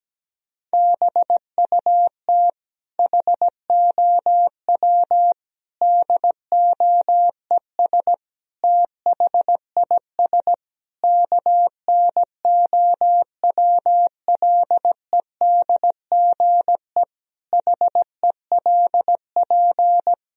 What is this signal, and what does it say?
Morse code, heard as a keyed tone, about 17 wpm, 700 Hz. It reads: BUT HOW DOES THIS KNOWLEDGE HELP
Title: Sentences from Top 1000 Words - Review Part 3 - 17wpm